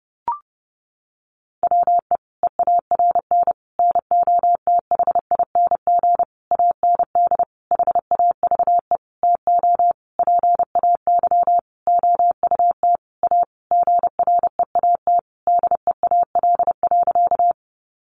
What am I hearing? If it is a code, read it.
Morse code, heard as a keyed tone, about 28 wpm, 700 Hz. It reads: WE EARN NOTHING AND HAVE TO PAY OUT A GREAT DEAL.
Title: Sentences from Top 1500 Words 30wpm